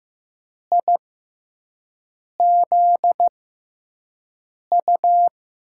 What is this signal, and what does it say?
Morse code, heard as a keyed tone, about 15 wpm, spaced as for 5 wpm, 700 Hz. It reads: IZU
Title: USA Call Signs - 6x Character Spacing - 15wpm